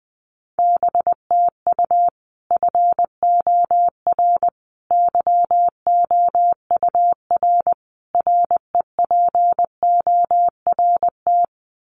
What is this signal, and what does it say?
Morse code, heard as a keyed tone, about 20 wpm, 700 Hz. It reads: BTU FOR YOUR REPORT
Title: Common QSO Phrases 20wpm